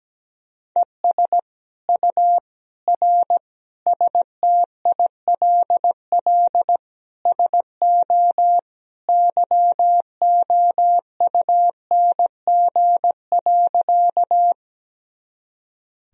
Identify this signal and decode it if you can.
Morse code, heard as a keyed tone, about 17 wpm, 700 Hz. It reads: ES U R STILL SO YOUNG.
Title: Sentences from Top 400 Words and Abbreviations 17wpm